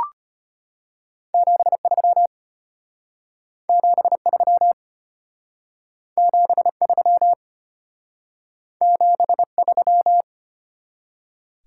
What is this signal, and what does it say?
Morse code, heard as a keyed tone, about 33 wpm, 700 Hz. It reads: 73 73 73 73